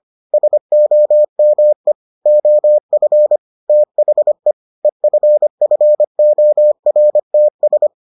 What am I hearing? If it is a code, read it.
Morse code, heard as a keyed tone, about 25 wpm, 600 Hz. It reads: SOME OF THE EFFORTS